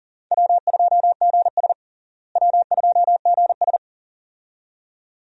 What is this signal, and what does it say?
Morse code, heard as a keyed tone, about 37 wpm, 700 Hz. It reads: W2GS W2GS
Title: USA Call Signs - 4 Characters - Repeated 2x - 40wpm